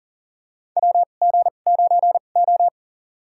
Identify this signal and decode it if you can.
Morse code, heard as a keyed tone, about 40 wpm, spaced as for 29 wpm, 700 Hz. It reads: WG9O
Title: USA Call Signs - 2x Character Spacing - 40wpm